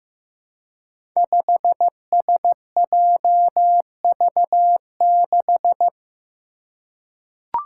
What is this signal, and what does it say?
Morse code, heard as a keyed tone, about 15 wpm, 700 Hz. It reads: 5SJV6
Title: Five Challenging Letter-Number Combinations 15wpm